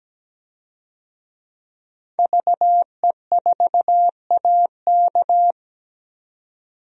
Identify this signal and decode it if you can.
Morse code, heard as a keyed tone, about 17 wpm, 700 Hz. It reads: VE4AK
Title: DXCC-100 Call Signs - 5 Characters - 17wpm